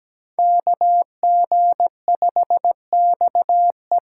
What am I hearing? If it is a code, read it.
Morse code, heard as a keyed tone, about 20 wpm, 700 Hz. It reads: KG5XE
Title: Speed Racing - USA Call Signs 17wpm